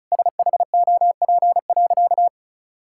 Morse code, heard as a keyed tone, about 36 wpm, 700 Hz. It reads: SHOP.
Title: Sentences from Top 1600 Words 35wpm